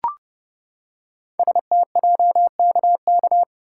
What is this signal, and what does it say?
Morse code, heard as a keyed tone, about 28 wpm, 700 Hz. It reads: STJKK
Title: Code Groups - 5 Letters - 30wpm